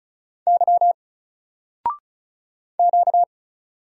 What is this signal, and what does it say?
Morse code, heard as a keyed tone, about 35 wpm, 700 Hz. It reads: Y Q
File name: Single-Letter-Number-35wpm